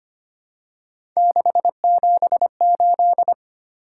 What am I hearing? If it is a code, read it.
Morse code, heard as a keyed tone, about 25 wpm, 700 Hz. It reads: 678
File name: Three Letter-Number Combinations - 25wpm